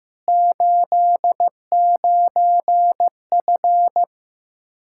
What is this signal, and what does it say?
Morse code, heard as a keyed tone, about 15 wpm, 700 Hz. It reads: 89F
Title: Practice QSO - Exchange 2 - Info - 15wpm